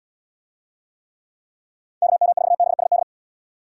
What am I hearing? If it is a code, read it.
Morse code, heard as a keyed tone, about 73 wpm, 700 Hz. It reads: KM4BSD